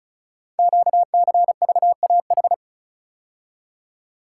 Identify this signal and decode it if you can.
Morse code, heard as a keyed tone, about 36 wpm, 700 Hz. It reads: QCVAH
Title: Code Groups - 5 Letters - 35wpm